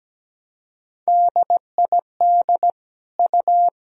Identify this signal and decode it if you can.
Morse code, heard as a keyed tone, about 17 wpm, 700 Hz. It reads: DID U
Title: Sentences from Top 100 Words and Abbreviations 17wpm